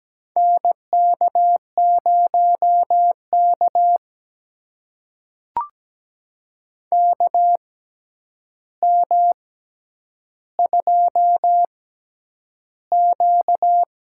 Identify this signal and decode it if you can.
Morse code, heard as a keyed tone, about 17 wpm, spaced as for 9 wpm, 700 Hz. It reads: NK0K KM2Q
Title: USA Call Signs - 4 Characters - 6x Character Spacing - 17wpm